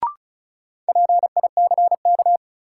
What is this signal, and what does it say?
Morse code, heard as a keyed tone, about 36 wpm, 700 Hz. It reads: PICK
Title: Sentences from Top 1000 Words - New Word Drill - 35wpm